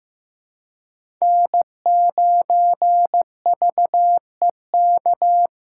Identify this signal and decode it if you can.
Morse code, heard as a keyed tone, about 16 wpm, 700 Hz. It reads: N9VEK
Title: USA Callsign DE Callsign 15wpm